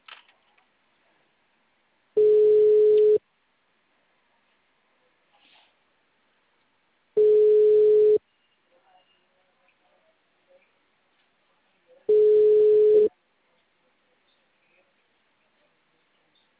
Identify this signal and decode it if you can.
Ringback tone (Europe, 425 Hz).